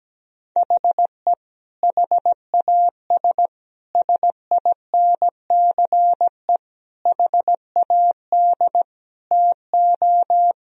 Morse code, heard as a keyed tone, about 17 wpm, 700 Hz. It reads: HE HAS SINCE HAD TO
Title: Sentences from Top 1000 Words - Review Part 2 - 17wpm